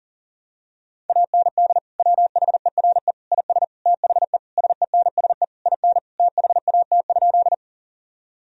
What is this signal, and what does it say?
Morse code, heard as a keyed tone, about 37 wpm, 700 Hz. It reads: AND WHERE IS THE SENSE IN THAT?
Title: Sentences from Top 1000 Words - Review Part 4 - 40wpm